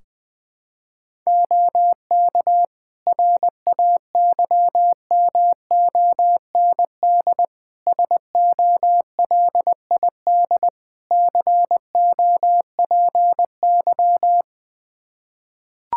Morse code, heard as a keyed tone, about 21 wpm, 700 Hz. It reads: OK RAYMOND SOLID COPY
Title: Practice QSO - Exchange 3 - More Info - 20wpm